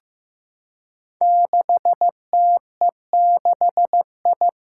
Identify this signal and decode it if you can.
Morse code, heard as a keyed tone, about 15 wpm, 700 Hz. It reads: 6TE6I
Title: Five Challenging Letter-Number Combinations 15wpm